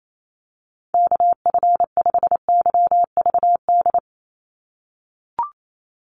Morse code, heard as a keyed tone, about 28 wpm, 700 Hz. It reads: KF5YVD
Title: USA Call Signs 28wpm